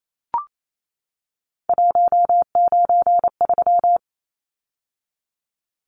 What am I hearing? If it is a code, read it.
Morse code, heard as a keyed tone, about 28 wpm, 700 Hz. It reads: 193
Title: Numbers - 3-Digit 28wpm